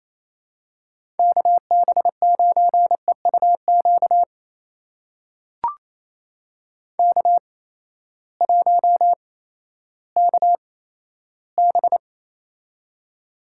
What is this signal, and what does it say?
Morse code, heard as a keyed tone, about 28 wpm, spaced as for 14 wpm, 700 Hz. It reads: KB9EUQ K1KB
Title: USA Call Signs - 8x Character Spacing - 28wpm